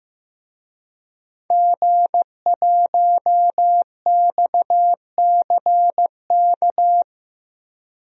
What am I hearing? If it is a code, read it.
Morse code, heard as a keyed tone, about 15 wpm, 700 Hz. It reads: G1XCK
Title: DXCC-100 Call Signs - 5 Characters - 15wpm